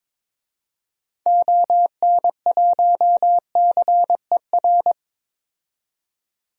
Morse code, heard as a keyed tone, about 22 wpm, 700 Hz. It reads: ON1CER
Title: DXCC-100 Call Signs - 6 Characters - 22wpm